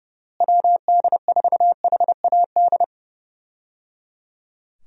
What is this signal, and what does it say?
Morse code, heard as a keyed tone, about 30 wpm, 700 Hz. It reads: WD4HAD
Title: USA Call Signs - 6 Characters - Repeated 3x - 30wpm